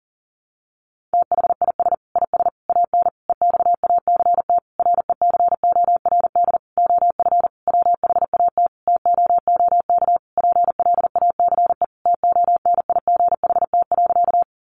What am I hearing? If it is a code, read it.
Morse code, heard as a keyed tone, about 40 wpm, 700 Hz. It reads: THIS IS AN EXACT RECORD OF WHAT TOOK PLACE TONIGHT.
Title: Sentences from Top 1300 Words 40wpm